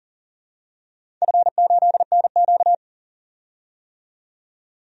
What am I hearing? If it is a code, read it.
Morse code, heard as a keyed tone, about 40 wpm, 700 Hz. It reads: F8NQ
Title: DXCC-100 Call Signs - 4 Characters - 40wpm